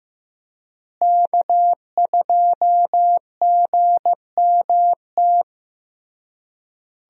Morse code, heard as a keyed tone, about 15 wpm, 700 Hz. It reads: K2GMT
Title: USA Call Signs - 5 Characters - Repeated 3x - 15wpm